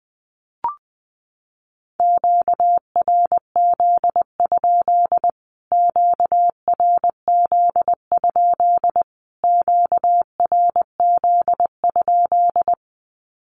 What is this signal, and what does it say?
Morse code, heard as a keyed tone, about 20 wpm, 700 Hz. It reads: QRZ? QRZ? QRZ?
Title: Q-Codes Repeated 3x 20wpm